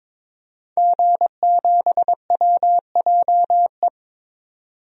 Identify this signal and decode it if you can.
Morse code, heard as a keyed tone, about 22 wpm, 700 Hz. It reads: G7WJE
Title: DXCC-100 Call Signs - 5 Characters - 22wpm